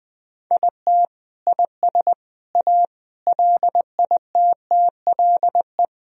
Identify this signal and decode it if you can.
Morse code, heard as a keyed tone, about 21 wpm, 700 Hz. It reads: IT IS A LITTLE
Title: Sentences from Top 1000 Words - Review Part 1 - 20wpm